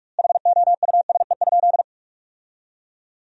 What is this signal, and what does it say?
Morse code, heard as a keyed tone, about 46 wpm, 700 Hz. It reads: HOUSE?
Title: 2x Word-Spacing - Sentences from Top 200 Words - 45wpm